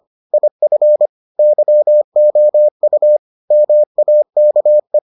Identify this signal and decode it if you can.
Morse code, heard as a keyed tone, about 25 wpm, 600 Hz. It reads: IF YOU MAKE